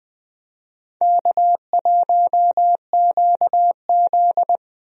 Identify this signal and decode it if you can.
Morse code, heard as a keyed tone, about 21 wpm, 700 Hz. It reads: K1QZ
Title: USA Call Signs - 4 Characters - Repeated 3x - 20wpm